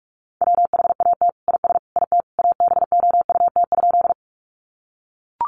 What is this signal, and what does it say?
Morse code, heard as a keyed tone, about 44 wpm, 700 Hz. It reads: WHAT IS IT ABOUT?
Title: Sentences from Top 100 Words 45wpm